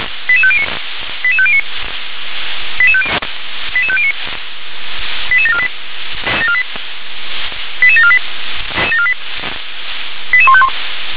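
Five-tone selective call (ZVEI), heard as quick runs of five tones, tones of about 70 ms. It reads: 88509 88509 88509 88748 88748 88151
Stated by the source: ZVEI five-tone selective call received on an SDR.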